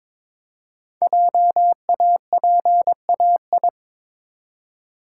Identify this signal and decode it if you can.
Morse code, heard as a keyed tone, about 23 wpm, 700 Hz. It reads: JAPAI
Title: Code Groups - 5 Letter:Numbers - 22wpm